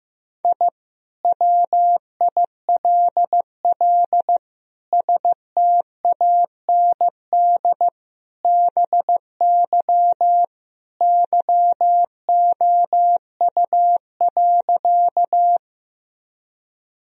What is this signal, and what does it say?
Morse code, heard as a keyed tone, about 15 wpm, 700 Hz. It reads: I WILL STAND BY YOU.
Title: Sentences from Top 1000 Words - Review Part 1 - 15wpm